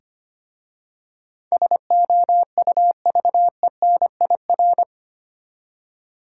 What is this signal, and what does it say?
Morse code, heard as a keyed tone, about 25 wpm, 690 Hz. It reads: SOUVENIR
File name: practice / 25wpm / Easy Words with Hints - 25wpm